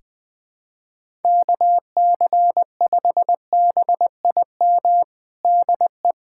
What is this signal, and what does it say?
Morse code, heard as a keyed tone, about 20 wpm, 700 Hz. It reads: KC5BIM DE
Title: USA Callsign DE Callsign 20wpm